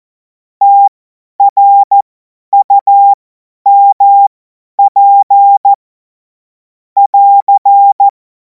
Morse code, heard as a keyed tone, about 14 wpm, spaced as for 10 wpm, 800 Hz. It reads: TRUMP <AR>
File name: Morsekurs / Witze / Witz 12. Mai 2026 50 F70